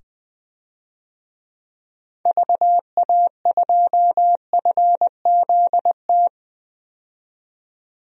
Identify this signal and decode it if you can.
Morse code, heard as a keyed tone, about 20 wpm, 700 Hz. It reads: VA2FZT
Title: DXCC-100 Call Signs - 6 Characters - 20wpm